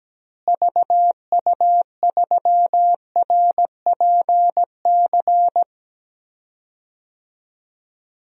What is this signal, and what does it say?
Morse code, heard as a keyed tone, about 17 wpm, 700 Hz. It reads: VU3RPC
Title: DXCC-100 Call Signs - 6 Characters - Repeated 2x - 17wpm